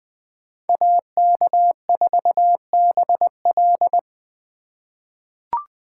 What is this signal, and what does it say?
Morse code, heard as a keyed tone, about 20 wpm, 700 Hz. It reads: AK4BL
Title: USA Call Signs - 2x Character Spacing - 20wpm